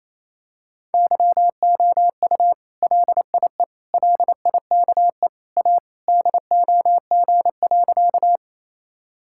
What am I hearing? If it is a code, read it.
Morse code, heard as a keyed tone, about 28 wpm, 700 Hz. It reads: YOU LIE LIKE A DOG.
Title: Sentences from Top 1600 Words 28wpm